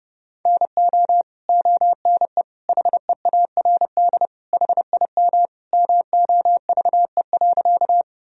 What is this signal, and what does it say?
Morse code, heard as a keyed tone, about 32 wpm, 700 Hz. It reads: NO ONE HEARD HIM MOVE.
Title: Sentences from Top 1000 Words - Review Part 1 - 30wpm